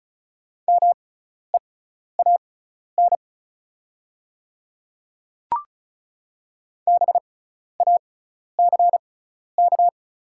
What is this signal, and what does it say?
Morse code, heard as a keyed tone, about 36 wpm, spaced as for 12 wpm, 690 Hz. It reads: MEAN BACK
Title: 6x Character Spacing - Top 500 Words - 35wpm